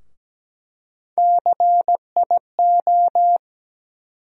Morse code, heard as a keyed tone, about 17 wpm, 700 Hz. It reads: CIO
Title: USA Call Sign Postfixes - 3 Letters 17wpm